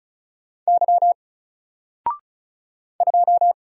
Morse code, heard as a keyed tone, about 36 wpm, 700 Hz. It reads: Y 2
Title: Single Character (Letter:Number:Punctuation:Prosign) 35wpm